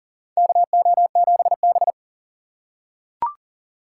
Morse code, heard as a keyed tone, about 43 wpm, 690 Hz. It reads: KO7B
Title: USA Call Signs - 4 Characters - 6x Character Spacing - 40wpm